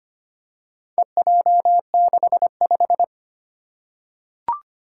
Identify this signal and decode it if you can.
Morse code, heard as a keyed tone, about 25 wpm, 700 Hz. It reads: EJ65
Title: Four Challenging Letter-Number Combinations 25wpm